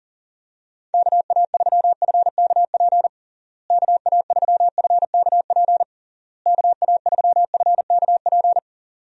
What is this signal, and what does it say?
Morse code, heard as a keyed tone, about 40 wpm, 700 Hz. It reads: KA3FKP KA3FKP KA3FKP
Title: USA Call Signs - 6 Characters - Repeated 3x - 40wpm